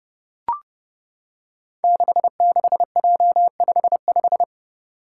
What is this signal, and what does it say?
Morse code, heard as a keyed tone, about 30 wpm, 700 Hz. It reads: 66J55
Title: Five Challenging Letter-Number Combinations 30wpm